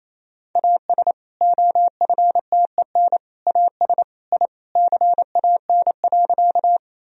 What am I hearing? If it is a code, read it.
Morse code, heard as a keyed tone, about 28 wpm, 700 Hz. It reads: AS OFTEN AS I CAN.
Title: Sentences from Top 1000 Words - Review Part 1 - 28wpm